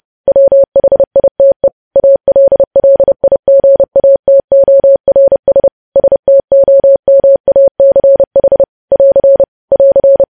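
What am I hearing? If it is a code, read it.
Morse code, heard as a keyed tone, about 30 wpm, 550 Hz. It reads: WHITE ALLIGATORS STOMACH <AR> <AR>
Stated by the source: News headlines sent in Morse at 30.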